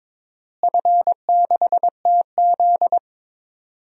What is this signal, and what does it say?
Morse code, heard as a keyed tone, about 22 wpm, 700 Hz. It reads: F6TZ
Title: DXCC-100 Call Signs - 4 Characters - Repeated 2x - 22wpm